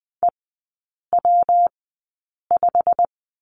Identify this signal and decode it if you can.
Morse code, heard as a keyed tone, about 20 wpm, spaced as for 8 wpm, 700 Hz. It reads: EW5
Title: 159 Lesson 26e - Letter g - Three Characters 20wpm